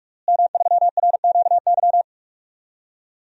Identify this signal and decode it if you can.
Morse code, heard as a keyed tone, about 44 wpm, 700 Hz. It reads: M3RQY
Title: DXCC-100 Call Signs - 5 Characters - Repeated 2x - 45wpm